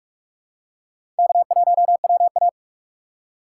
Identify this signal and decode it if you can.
Morse code, heard as a keyed tone, about 46 wpm, 700 Hz. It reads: K1WA
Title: USA Call Signs - 2x Character Spacing - 45wpm